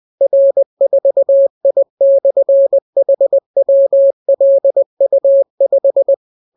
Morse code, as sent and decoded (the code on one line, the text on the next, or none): .-. ....- .. -..-. .... .-- .-.. ..- .....
R4I/HWLU5